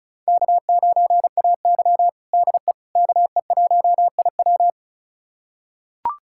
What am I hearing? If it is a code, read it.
Morse code, heard as a keyed tone, about 36 wpm, 700 Hz. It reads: K9AY DE KE1IW
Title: USA Callsign DE Callsign 35wpm